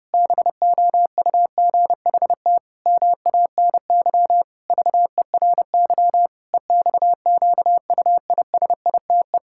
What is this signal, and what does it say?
Morse code, heard as a keyed tone, about 30 wpm, 700 Hz. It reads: BOUGHT MANY VERY EXQUISITE
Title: Pangram Challenge v2 - 30wpm